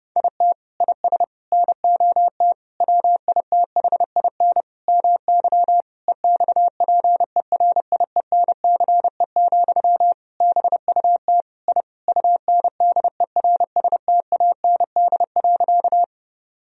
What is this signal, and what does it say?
Morse code, heard as a keyed tone, about 30 wpm, 700 Hz. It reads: IT IS NOT WITHIN MY EXPERIENCE, BUT I UNDERSTAND.
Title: Sentences from Top 1000 Words - Review Part 4 - 30wpm